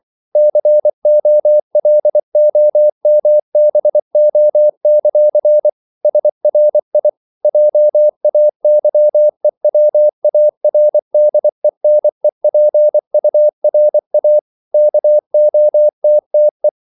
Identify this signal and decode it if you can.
Morse code, heard as a keyed tone, about 24 wpm, 600 Hz. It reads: COLOMBO; SRI JAYEWARDENEPURA KOTTE
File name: CW / Practice / NationalCapitals24